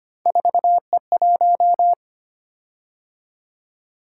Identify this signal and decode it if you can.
Morse code, heard as a keyed tone, about 25 wpm, 700 Hz. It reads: 4E1
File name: Three Letter-Number Combinations - 25wpm